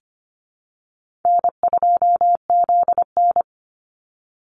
Morse code, heard as a keyed tone, about 25 wpm, 700 Hz. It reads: N2ZN
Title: Call Signs Repeated 2x - 25wpm